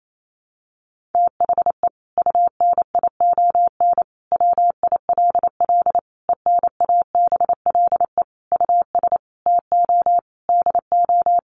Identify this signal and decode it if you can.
Morse code, heard as a keyed tone, about 28 wpm, 700 Hz. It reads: THE UNION WILL ENABLE US TO DO
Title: Sentences from Top 1500 Words 28wpm